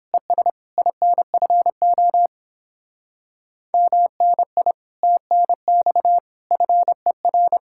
Morse code, heard as a keyed tone, about 29 wpm, 700 Hz. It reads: ES INFO MNI TNX FER